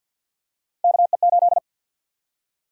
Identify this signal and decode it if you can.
Morse code, heard as a keyed tone, about 47 wpm, 700 Hz. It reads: KE8
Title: USA Call Sign Prefixes - 50wpm